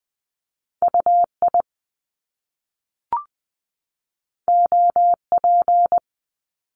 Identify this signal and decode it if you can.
Morse code, heard as a keyed tone, about 20 wpm, 700 Hz. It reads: UI OP
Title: Top Two Letter Combinations in Words 20wpm